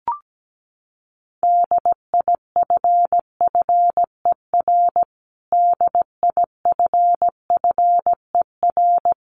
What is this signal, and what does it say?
Morse code, heard as a keyed tone, about 17 wpm, 700 Hz. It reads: DIFFER DIFFER
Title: Top 200 Words Repeated 17wpm